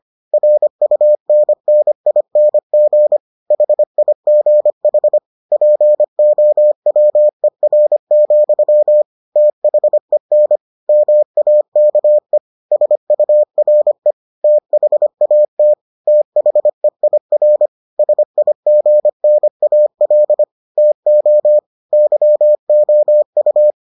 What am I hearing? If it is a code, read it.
Morse code, heard as a keyed tone, about 25 wpm, 600 Hz. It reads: RUNNING HIGH POWER, THEN MAKE SURE THAT THEIR SIGNAL TO YOU